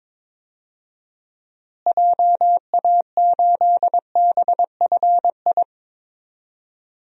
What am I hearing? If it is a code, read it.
Morse code, heard as a keyed tone, about 22 wpm, 700 Hz. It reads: JA8BFI
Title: DXCC-100 Call Signs - 6 Characters - Repeated 3x - 22wpm